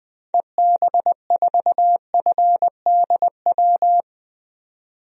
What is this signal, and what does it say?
Morse code, heard as a keyed tone, about 20 wpm, 700 Hz. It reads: EB4FDW
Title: DXCC-100 Call Signs - 6 Characters - 20wpm